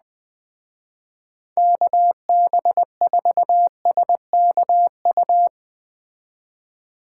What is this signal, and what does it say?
Morse code, heard as a keyed tone, about 21 wpm, 700 Hz. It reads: KB4SKU